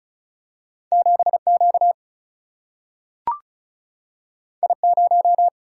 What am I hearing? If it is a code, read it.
Morse code, heard as a keyed tone, about 36 wpm, 700 Hz. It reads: 7 Q I 0
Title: Single-Letter Single-Number 35wpm